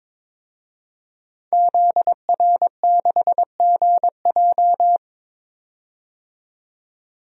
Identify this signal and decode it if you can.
Morse code, heard as a keyed tone, about 22 wpm, 700 Hz. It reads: ZR6GJ
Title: DXCC-100 Call Signs - 5 Characters - 22wpm